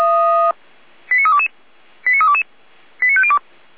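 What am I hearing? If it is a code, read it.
Five-tone selective call (ZVEI), heard as quick runs of five tones, tones of about 70 ms. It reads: 89322 89322 89502
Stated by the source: ZVEI five-tone call received on SDR, feuerwehr-orlamuende.org (fire brigade Orlamünde).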